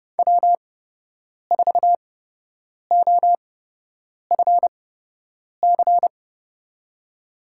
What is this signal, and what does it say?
Morse code, heard as a keyed tone, about 30 wpm, spaced as for 8 wpm, 700 Hz. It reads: W4OFC
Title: USA Call Signs - 8x Character Spacing - 30wpm